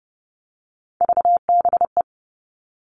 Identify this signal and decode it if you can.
Morse code, heard as a keyed tone, about 30 wpm, 700 Hz. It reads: VBE
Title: Three Challenging Letter-Number Combinations 30wpm